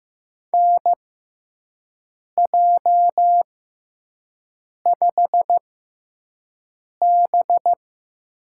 Morse code, heard as a keyed tone, about 15 wpm, spaced as for 5 wpm, 700 Hz. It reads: NJ5B
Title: USA Call Signs - 6x Character Spacing - 15wpm